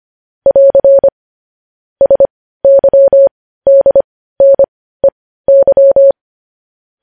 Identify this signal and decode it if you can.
Morse code, heard as a keyed tone, about 25 wpm, spaced as for 15 wpm, 550 Hz. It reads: <AR> SYDNEY